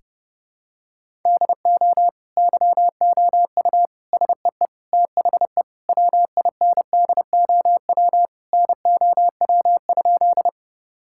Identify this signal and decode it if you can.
Morse code, heard as a keyed tone, about 30 wpm, 700 Hz. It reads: DO YOU SEE THE WINDOW NOW?
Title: Sentences from Top 1600 Words 30wpm